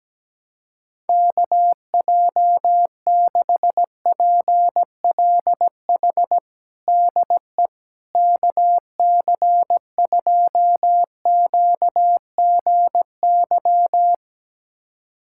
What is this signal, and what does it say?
Morse code, heard as a keyed tone, about 17 wpm, 700 Hz. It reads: KJ6PLH DE KC2QGY
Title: USA Callsign DE Callsign - 6 Characters - 17wpm